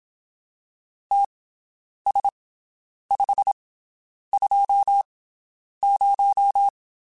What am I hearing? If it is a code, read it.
Morse code, heard as a keyed tone, about 27 wpm, spaced as for 9 wpm, 800 Hz. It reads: TS520
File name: QSOs Spaced